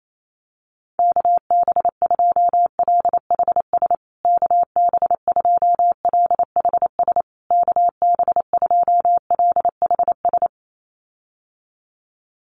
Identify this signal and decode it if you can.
Morse code, heard as a keyed tone, about 28 wpm, 700 Hz. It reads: KB2LHS KB2LHS KB2LHS
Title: USA Call Signs Repeated 3x 28wpm